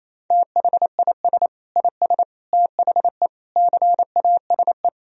Morse code, heard as a keyed tone, about 28 wpm, 700 Hz. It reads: THIS IS THE CASE
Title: Sentences from Top 1000 Words - Review Part 3 - 28wpm